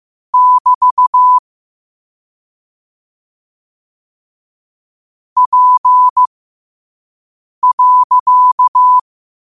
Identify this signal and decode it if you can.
Morse code, heard as a keyed tone, about 15 wpm, spaced as for 5 wpm, 1000 Hz. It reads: <BT> P.
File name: qotdfiles / 5wpm